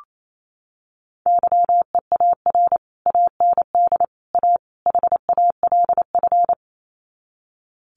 Morse code, heard as a keyed tone, about 28 wpm, 700 Hz. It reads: YEAR AND A HALF
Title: Sets of 4 Words 28wpm